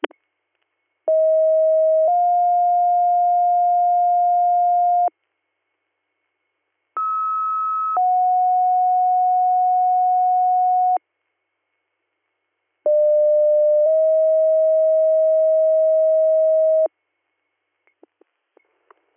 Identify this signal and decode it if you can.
Two-tone page, heard as a one-second tone, then a three-second tone, about 640 then 710 Hz, about 1290 then 730 Hz, about 600 then 640 Hz. It is heard from 1.1 to 16.9 s.